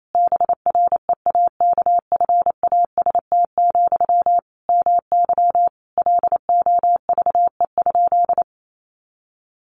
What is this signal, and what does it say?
Morse code, heard as a keyed tone, about 28 wpm, 700 Hz. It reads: BREAKFAST, MY LOVE?
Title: Sentences from Top 1500 Words 28wpm